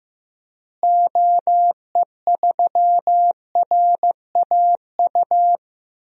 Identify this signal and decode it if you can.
Morse code, heard as a keyed tone, about 16 wpm, 700 Hz. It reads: OE3RAU
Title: DXCC-100 Call Signs - 6 Characters - Repeated 3x - 15wpm